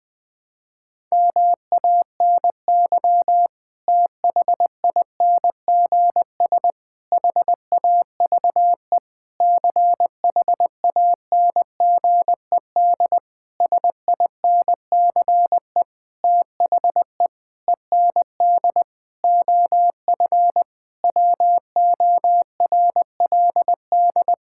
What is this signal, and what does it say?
Morse code, heard as a keyed tone, about 20 wpm, 700 Hz. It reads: MANY THINGS HAVE CHANGED SINCE THE END OF WORLD
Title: Sentences with 7-Word Phrases - Spoken Beginning - 20wpm